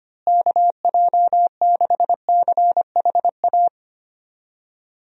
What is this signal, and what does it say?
Morse code, heard as a keyed tone, about 25 wpm, 700 Hz. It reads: KJ6CHA
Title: USA Call Signs - 6 Characters - Repeated 2x - 25wpm